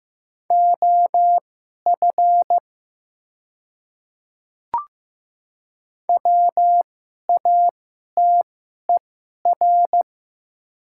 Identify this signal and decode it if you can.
Morse code, heard as a keyed tone, about 15 wpm, spaced as for 11 wpm, 700 Hz. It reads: OF WATER
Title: 2x Character Spacing - Top 100 Words - 15wpm